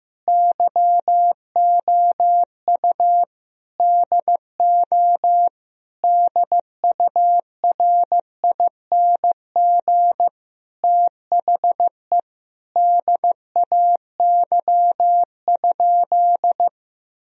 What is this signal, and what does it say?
Morse code, heard as a keyed tone, about 15 wpm, 700 Hz. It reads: YOU DO DURING THE DAY?
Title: Sentences from Top 1000 Words - Review Part 2 - 15wpm